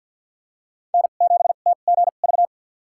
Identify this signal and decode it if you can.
Morse code, heard as a keyed tone, about 53 wpm, spaced as for 37 wpm, 700 Hz. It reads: N7TGV
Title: USA Call Signs - 5 Characters - 2x Character Spacing - 50wpm